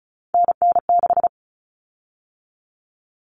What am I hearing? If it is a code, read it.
Morse code, heard as a keyed tone, about 34 wpm, 700 Hz. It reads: NN6
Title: Common Cut Numbers - 3-Digit 35wpm